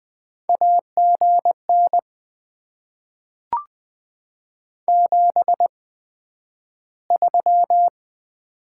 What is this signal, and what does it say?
Morse code, heard as a keyed tone, about 20 wpm, spaced as for 11 wpm, 700 Hz. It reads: AGN 73
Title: Top 20 QSO Elements - 8x Character Spacing - 20wpm